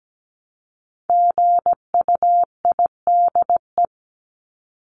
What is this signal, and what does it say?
Morse code, heard as a keyed tone, about 17 wpm, 700 Hz. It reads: GUIDE
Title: Top 501-750 Words 17wpm